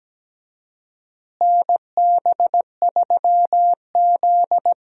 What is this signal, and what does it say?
Morse code, heard as a keyed tone, about 17 wpm, 700 Hz. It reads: NB3Z
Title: USA Call Signs - 4 Characters - Repeated 3x - 17wpm